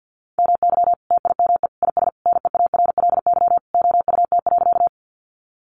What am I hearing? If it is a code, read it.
Morse code, heard as a keyed tone, about 50 wpm, 700 Hz. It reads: MY TIME IS NEARLY OUT.
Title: Sentences from Top 1300 Words 50wpm